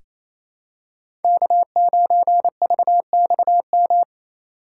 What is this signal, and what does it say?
Morse code, heard as a keyed tone, about 28 wpm, 700 Hz. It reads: K9VXM
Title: USA Call Signs - 5 Characters - 2x Character Spacing - 28wpm